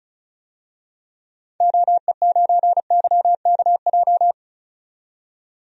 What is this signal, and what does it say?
Morse code, heard as a keyed tone, about 36 wpm, 700 Hz. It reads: OE9YKJ
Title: DXCC-100 Call Signs - 6 Characters - 35wpm